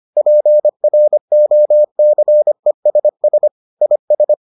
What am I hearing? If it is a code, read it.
Morse code, heard as a keyed tone, about 25 wpm, 600 Hz. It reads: PROCESS IS